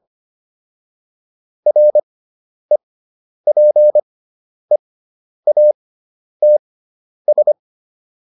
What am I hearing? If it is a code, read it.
Morse code, heard as a keyed tone, about 25 wpm, spaced as for 10 wpm, 600 Hz. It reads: REPEATS